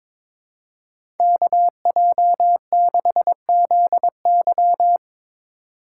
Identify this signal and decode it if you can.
Morse code, heard as a keyed tone, about 22 wpm, 700 Hz. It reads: KJ6ZY
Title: USA Call Signs - 8x Character Spacing - 22wpm